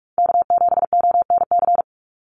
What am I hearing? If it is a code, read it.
Morse code, heard as a keyed tone, about 44 wpm, 700 Hz. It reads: K7ONC
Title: USA Call Signs 45wpm